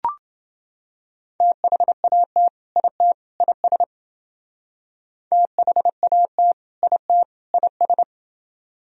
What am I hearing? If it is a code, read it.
Morse code, heard as a keyed tone, about 28 wpm, 700 Hz. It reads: THAT IT IS THAT IT IS